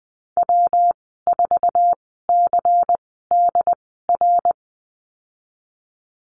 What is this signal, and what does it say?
Morse code, heard as a keyed tone, about 20 wpm, spaced as for 14 wpm, 700 Hz. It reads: W4CDR
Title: 140 Lesson 23g - Letter f - Call Signs 20wpm